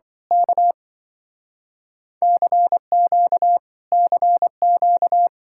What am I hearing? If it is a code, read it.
Morse code, heard as a keyed tone, about 26 wpm, 700 Hz. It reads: K CQ CQ